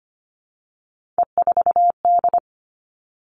Morse code, heard as a keyed tone, about 25 wpm, 700 Hz. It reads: E4D
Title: Three Challenging Letter-Number Combinations 25wpm